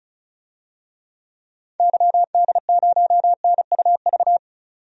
Morse code, heard as a keyed tone, about 34 wpm, 700 Hz. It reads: YD0NUV
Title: DXCC-100 Call Signs - 6 Characters - 35wpm